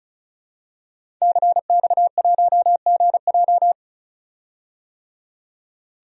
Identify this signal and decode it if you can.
Morse code, heard as a keyed tone, about 34 wpm, 700 Hz. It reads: CX1GJ
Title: DXCC-100 Call Signs - 5 Characters - 35wpm